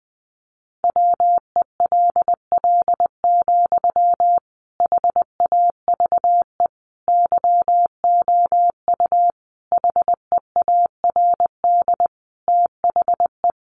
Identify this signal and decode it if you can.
Morse code, heard as a keyed tone, about 20 wpm, 700 Hz. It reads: WELL, HAVE YOU HEARD THE
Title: Sentences from Top 800 Words 20wpm